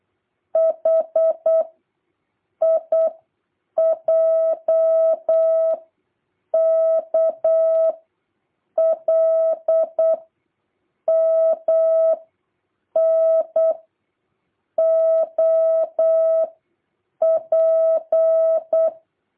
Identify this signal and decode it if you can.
Morse code, heard as a keyed tone, about 8 wpm, 650 Hz. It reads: H I J K L M N O P